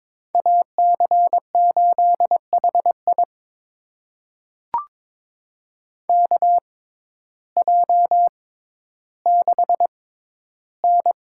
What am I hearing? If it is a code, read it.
Morse code, heard as a keyed tone, about 22 wpm, spaced as for 12 wpm, 700 Hz. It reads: AC8HI KJ6N
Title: USA Call Signs - 6x Character Spacing - 22wpm